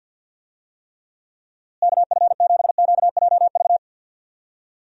Morse code, heard as a keyed tone, about 47 wpm, 700 Hz. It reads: KF7QJV
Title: USA Call Signs - 6 Characters - Repeated 3x - 50wpm